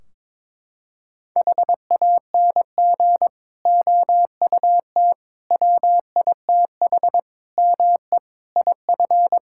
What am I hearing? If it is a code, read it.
Morse code, heard as a keyed tone, about 22 wpm, 700 Hz. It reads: HANG OUT WITH ME IF
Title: Sentences from Top 1600 Words 22wpm